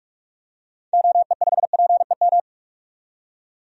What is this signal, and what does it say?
Morse code, heard as a keyed tone, about 46 wpm, 700 Hz. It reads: OE5PEM
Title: DXCC-100 Call Signs - 6 Characters - 45wpm